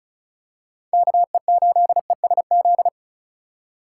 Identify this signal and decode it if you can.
Morse code, heard as a keyed tone, about 36 wpm, 700 Hz. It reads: KE8ESZ
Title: USA Call Signs - 4x Character Spacing - 35wpm